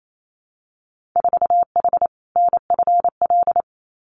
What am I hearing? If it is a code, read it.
Morse code, heard as a keyed tone, about 28 wpm, 700 Hz. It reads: 4H NFL
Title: Winter Field Day Exchanges Encore 28wpm